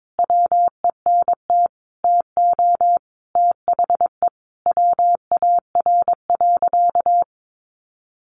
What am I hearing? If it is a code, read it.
Morse code, heard as a keyed tone, about 22 wpm, 700 Hz. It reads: WENT TO THE WAR.
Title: Sentences from Top 400 Words 22wpm